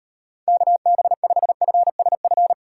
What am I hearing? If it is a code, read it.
Morse code, heard as a keyed tone, about 37 wpm, 700 Hz. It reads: KB5FSF